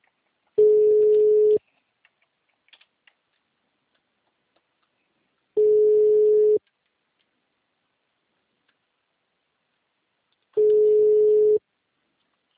Ringback tone (Europe, 425 Hz).